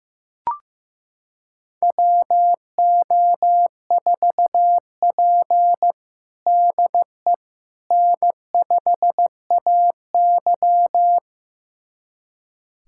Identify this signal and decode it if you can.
Morse code, heard as a keyed tone, about 16 wpm, 700 Hz. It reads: WO4P DE N5AY
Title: USA Callsign DE Callsign - 4 Characters - 15wpm